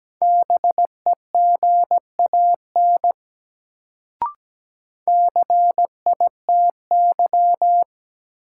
Morse code, heard as a keyed tone, about 17 wpm, 700 Hz. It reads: BEGAN CITY
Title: Sentences from Top 300 Words - New Word Drill - 17wpm